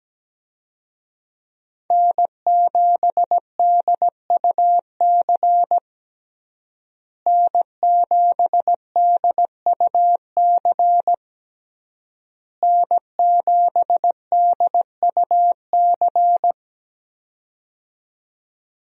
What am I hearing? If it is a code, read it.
Morse code, heard as a keyed tone, about 17 wpm, 700 Hz. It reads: N7DUC N7DUC N7DUC
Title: USA Call Signs - 5 Characters - Repeated 3x - 17wpm